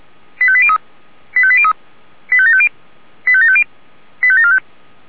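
Five-tone selective call (ZVEI), heard as quick runs of five tones, tones of about 70 ms. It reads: 85803 85803 85750 85750 85745
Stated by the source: ZVEI five-tone selective call received via SDR.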